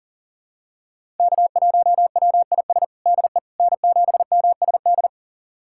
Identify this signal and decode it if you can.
Morse code, heard as a keyed tone, about 39 wpm, 700 Hz. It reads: K1WIS DE N7MSD
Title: USA Callsign DE Callsign 40wpm